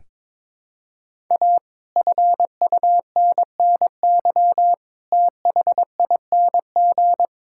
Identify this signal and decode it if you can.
Morse code, heard as a keyed tone, about 22 wpm, 700 Hz. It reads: A FUNNY THING
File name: practice / 22wpm / Sets of 8 Words - 22wpm